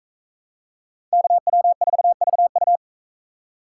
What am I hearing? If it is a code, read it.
Morse code, heard as a keyed tone, about 39 wpm, 700 Hz. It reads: KW4VU